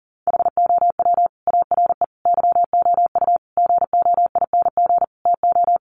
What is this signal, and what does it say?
Morse code, heard as a keyed tone, about 40 wpm, 700 Hz. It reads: HOW ARE YOU GOING TO
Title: Sets of 5 Words 40wpm